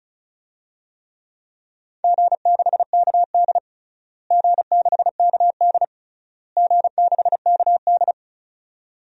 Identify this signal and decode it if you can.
Morse code, heard as a keyed tone, about 36 wpm, 700 Hz. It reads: G6KD G6KD G6KD
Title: DXCC-100 Call Signs - 4 Characters - Repeated 3x - 35wpm